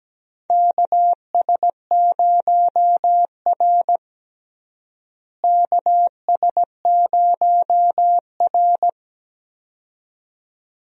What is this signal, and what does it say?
Morse code, heard as a keyed tone, about 17 wpm, 700 Hz. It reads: KS0R KS0R
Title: USA Call Signs - 4 Characters - Repeated 2x - 17wpm